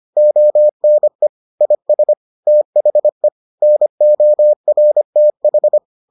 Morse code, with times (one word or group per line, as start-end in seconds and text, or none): 0.2-1.3 ONE
1.6-2.1 IS
2.5-3.3 THE
3.6-5.8 NORTH